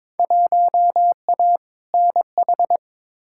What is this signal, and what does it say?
Morse code, heard as a keyed tone, about 22 wpm, 700 Hz. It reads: 1A NH
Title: Field Day Exchanges Encore - 22wpm